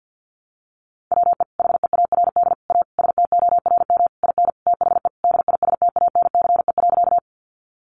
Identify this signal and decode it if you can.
Morse code, heard as a keyed tone, about 50 wpm, 700 Hz. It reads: WE HEARD A STORM IN THE DISTANCE.
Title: Sentences from Top 1500 Words 50wpm